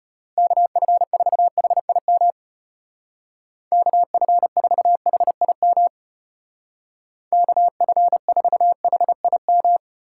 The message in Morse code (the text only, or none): KF4HIM KF4HIM KF4HIM